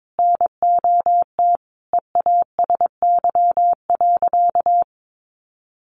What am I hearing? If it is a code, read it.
Morse code, heard as a keyed tone, about 22 wpm, 700 Hz. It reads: NOT EASY.
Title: Sentences from Top 1500 Words 22wpm